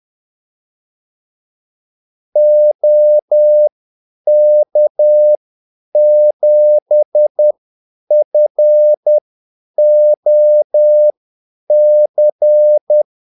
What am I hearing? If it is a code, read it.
Morse code, heard as a keyed tone, about 10 wpm, spaced as for 8 wpm, 600 Hz. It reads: OK7FOC
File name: Callsigns-10-8wpm